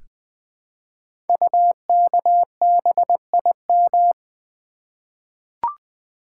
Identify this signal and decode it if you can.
Morse code, heard as a keyed tone, about 20 wpm, 700 Hz. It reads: UKBIM
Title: Code Groups - 5 Letters - 20wpm